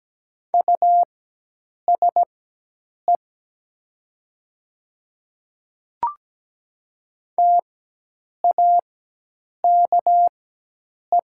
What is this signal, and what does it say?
Morse code, heard as a keyed tone, about 17 wpm, spaced as for 8 wpm, 700 Hz. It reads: USE TAKE
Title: 4x Character Spacing - Top 200 Words - 17wpm